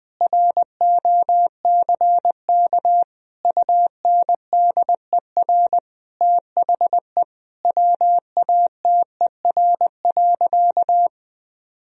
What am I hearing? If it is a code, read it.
Morse code, heard as a keyed tone, about 21 wpm, 700 Hz. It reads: ROCK UNDER THE WATER.
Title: Sentences from Top 1000 Words - Review Part 2 - 20wpm